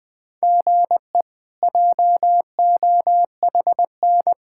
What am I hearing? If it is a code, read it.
Morse code, heard as a keyed tone, about 20 wpm, 700 Hz. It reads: GE JOHN
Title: K1USN SST QSO Practice - 20wpm